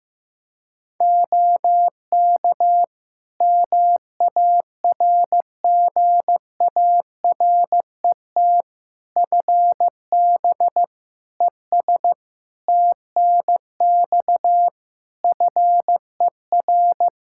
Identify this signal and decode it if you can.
Morse code, heard as a keyed tone, about 16 wpm, 700 Hz. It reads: OK MARGARET FB ES TNX FER
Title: Practice QSO - Exchange 2 - Info - 15wpm